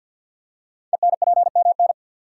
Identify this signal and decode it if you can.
Morse code, heard as a keyed tone, about 53 wpm, 700 Hz. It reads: ENPMN
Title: Code Groups - 5 Letter:Numbers - 50wpm